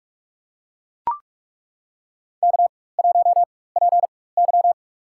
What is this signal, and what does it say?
Morse code, heard as a keyed tone, about 41 wpm, spaced as for 20 wpm, 700 Hz. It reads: K1PY
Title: USA Call Signs - 4 Characters - 4x Character Spacing - 45wpm